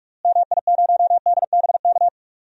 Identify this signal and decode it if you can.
Morse code, heard as a keyed tone, about 46 wpm, 700 Hz. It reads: MI0DBK
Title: DXCC-100 Call Signs - 6 Characters - Repeated 2x - 45wpm